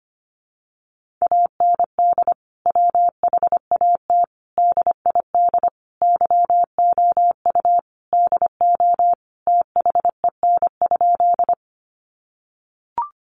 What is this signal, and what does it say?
Morse code, heard as a keyed tone, about 25 wpm, 700 Hz. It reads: AND WHAT DID YOU DO THEN?
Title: Sentences from Top 100 Words 25wpm